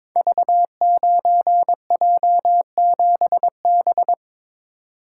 Morse code, heard as a keyed tone, about 22 wpm, 700 Hz. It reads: V9J7B
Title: Code Groups - 5 Letter:Numbers - 22wpm